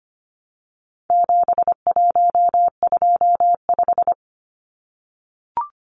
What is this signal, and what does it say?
Morse code, heard as a keyed tone, about 25 wpm, 700 Hz. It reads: 7125
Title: Numbers - 4-Digit 25wpm